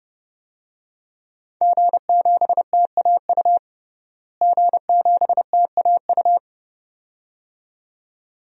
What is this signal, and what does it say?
Morse code, heard as a keyed tone, about 32 wpm, 700 Hz. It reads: G7TAU G7TAU
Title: DXCC-100 Call Signs - 5 Characters - Repeated 2x - 30wpm